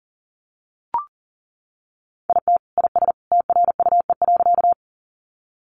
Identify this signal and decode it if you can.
Morse code, heard as a keyed tone, about 40 wpm, 700 Hz. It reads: IT IS TRUE.
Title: Sentences from Top 400 Words 40wpm